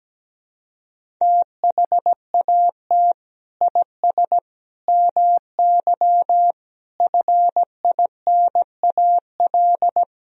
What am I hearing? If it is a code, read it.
Morse code, heard as a keyed tone, about 17 wpm, 700 Hz. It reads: THAT IS MY FINAL
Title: Sentences from Top 1000 Words - Review Part 2 - 17wpm